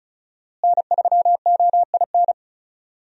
Morse code, heard as a keyed tone, about 36 wpm, 700 Hz. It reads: N3OIN
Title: USA Call Signs - 5 Characters - Repeated 3x - 35wpm